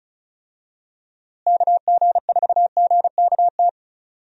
Morse code, heard as a keyed tone, about 34 wpm, 700 Hz. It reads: KG4GKT
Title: USA Call Signs - 6 Characters - Repeated 2x - 35wpm